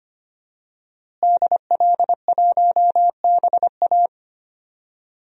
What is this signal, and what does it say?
Morse code, heard as a keyed tone, about 25 wpm, 700 Hz. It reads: DL1BA
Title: DXCC-100 Call Signs - 5 Characters - Repeated 2x - 25wpm